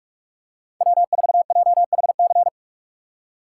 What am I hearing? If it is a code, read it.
Morse code, heard as a keyed tone, about 45 wpm, 700 Hz. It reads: W4JHC
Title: USA Call Signs - 5 Characters - 4x Character Spacing - 45wpm